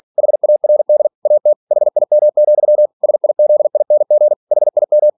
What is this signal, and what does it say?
Morse code, heard as a keyed tone, about 46 wpm, 600 Hz. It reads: HARD AT HIM, SIZING HIM
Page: Never 47wpm